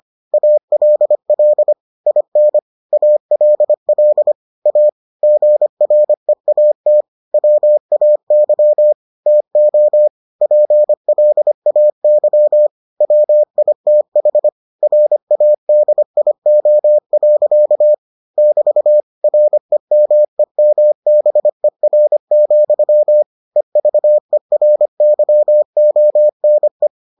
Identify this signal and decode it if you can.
Morse code, heard as a keyed tone, about 25 wpm, 600 Hz. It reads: ALL IN ALL A GREAT WAY TO PLAY WITH RADIO. <BT> REMEMBER, EVERYONE